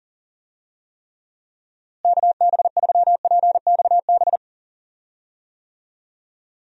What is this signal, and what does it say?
Morse code, heard as a keyed tone, about 40 wpm, 700 Hz. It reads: KB3PXB
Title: USA Call Signs - 6 Characters - Repeated 2x - 40wpm